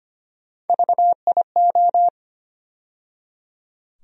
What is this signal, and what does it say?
Morse code, heard as a keyed tone, about 25 wpm, 700 Hz. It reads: VIO
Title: USA Call Sign Postfixes - 3 Letters 25wpm